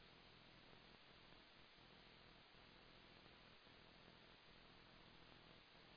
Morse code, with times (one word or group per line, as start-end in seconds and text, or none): none